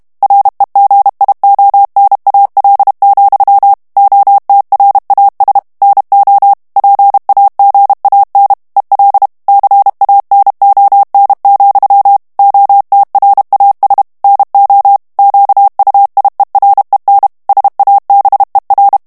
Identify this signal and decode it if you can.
Morse code, heard as a keyed tone, about 32 wpm, 800 Hz. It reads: REGIONAL, OTRAS NO PAGAN EL CANON, OTRAS NO QUIEREN SABER
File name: morse mp3 / ESPAGNOL TEXTE EN CLAIR 1 VITESSE 30 MOTS MINUTES